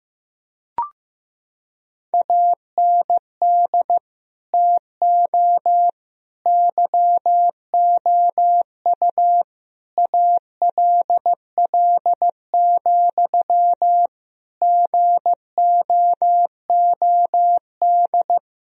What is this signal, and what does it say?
Morse code, heard as a keyed tone, about 16 wpm, 700 Hz. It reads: AND TO YOU ALL, GOOD
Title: Sentences from Top 1000 Words - Review Part 3 - 15wpm